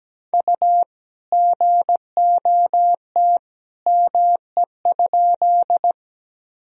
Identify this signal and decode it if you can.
Morse code, heard as a keyed tone, about 17 wpm, 700 Hz. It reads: U GOT ME?
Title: Sentences from Top 300 Words and Abbreviations 17wpm